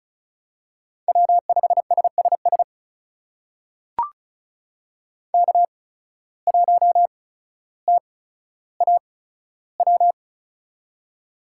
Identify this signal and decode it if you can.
Morse code, heard as a keyed tone, about 35 wpm, spaced as for 15 wpm, 700 Hz. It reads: W5SSS K1TAW
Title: USA Call Signs - 5 Characters - 8x Character Spacing - 35wpm